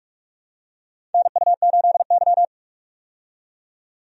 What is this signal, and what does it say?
Morse code, heard as a keyed tone, about 46 wpm, 700 Hz. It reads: NU8Y
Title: USA Call Signs - 4 Characters - 45wpm